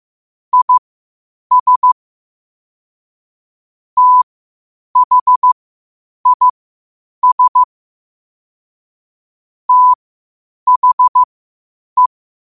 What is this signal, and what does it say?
Morse code, heard as a keyed tone, about 15 wpm, spaced as for 8 wpm, 1000 Hz. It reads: IS THIS THE
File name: qotdfiles / 8wpm